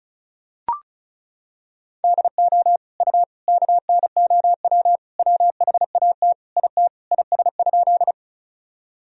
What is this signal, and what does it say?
Morse code, heard as a keyed tone, about 36 wpm, 700 Hz. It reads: DO U KNOW WHAT IT IS?
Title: Sentences from Top 100 Words and Abbreviations 35wpm